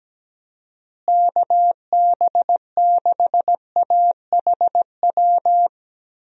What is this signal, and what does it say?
Morse code, heard as a keyed tone, about 17 wpm, 700 Hz. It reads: KB6AHW
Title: USA Call Signs - 6 Characters - Repeated 3x - 17wpm